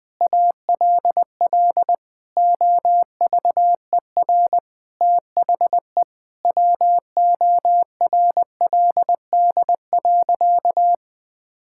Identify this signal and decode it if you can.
Morse code, heard as a keyed tone, about 19 wpm, 700 Hz. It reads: ALL OVER THE WORLD.
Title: Sentences with 5-Word Phrases - 20wpm